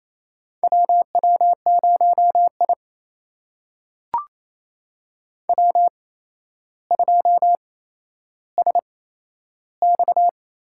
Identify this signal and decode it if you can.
Morse code, heard as a keyed tone, about 28 wpm, spaced as for 12 wpm, 700 Hz. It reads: WW0I W2SX
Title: USA Call Signs - 8x Character Spacing - 28wpm